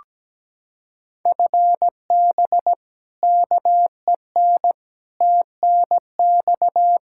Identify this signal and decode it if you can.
Morse code, heard as a keyed tone, about 17 wpm, 700 Hz. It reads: FB KEN TNX
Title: Practice QSO - Exchange 3 - More Info - 17wpm